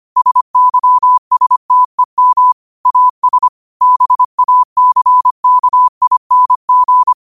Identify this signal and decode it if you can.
Morse code, heard as a keyed tone, about 25 wpm, 1000 Hz. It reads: SYSTEM AS BACKING
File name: qotdfiles / 25wpm